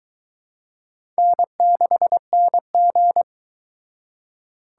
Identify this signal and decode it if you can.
Morse code, heard as a keyed tone, about 21 wpm, 700 Hz. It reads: N6NG